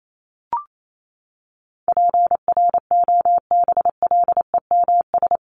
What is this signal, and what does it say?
Morse code, heard as a keyed tone, about 28 wpm, 700 Hz. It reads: PROBLEMS
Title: ES:S Suffix - Top 100 Words with Base Words 28wpm